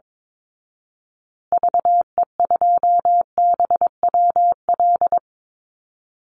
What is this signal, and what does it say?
Morse code, heard as a keyed tone, about 22 wpm, 700 Hz. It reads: VE2BWL
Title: Call Signs Repeated 3x - 22wpm